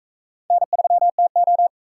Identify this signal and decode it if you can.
Morse code, heard as a keyed tone, about 44 wpm, 700 Hz. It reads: N3TO